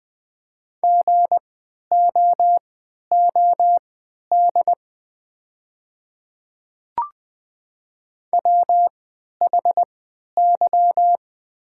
Morse code, heard as a keyed tone, about 20 wpm, spaced as for 11 wpm, 700 Hz. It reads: GOOD WHY
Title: 3x Character Spacing - Top 200 Words - 20wpm